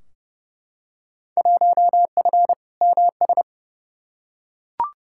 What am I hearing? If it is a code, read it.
Morse code, heard as a keyed tone, about 30 wpm, 700 Hz. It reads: 1F MS
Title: Field Day Exchanges Encore - 30wpm